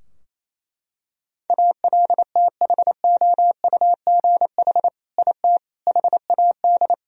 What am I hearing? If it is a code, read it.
Morse code, heard as a keyed tone, about 27 wpm, 700 Hz. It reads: ALTHOUGH IT HAD
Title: Sentences from Top 1000 Words - Review Part 3 - 28wpm